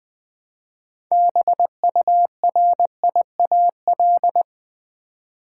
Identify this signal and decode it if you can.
Morse code, heard as a keyed tone, about 21 wpm, 700 Hz. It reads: BURIAL
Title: Build A Word - 2 Letters At A Time - 6-Letter Words - 20wpm